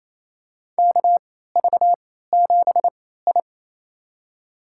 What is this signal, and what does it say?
Morse code, heard as a keyed tone, about 27 wpm, spaced as for 16 wpm, 700 Hz. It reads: KV7I
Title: USA Call Signs - 4 Characters - 3x Character Spacing - 28wpm